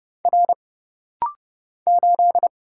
Morse code, heard as a keyed tone, about 30 wpm, 700 Hz. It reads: R 8
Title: Single Letter-Number - Rapid-Fire 30wpm